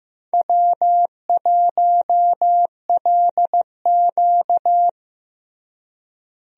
Morse code, heard as a keyed tone, about 15 wpm, 700 Hz. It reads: W1LQ
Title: USA Call Signs - 4 Characters - Repeated 3x - 15wpm